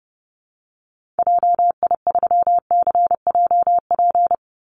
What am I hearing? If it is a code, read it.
Morse code, heard as a keyed tone, about 30 wpm, 700 Hz. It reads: JI3CJP
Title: Call Signs Repeated 3x - 30wpm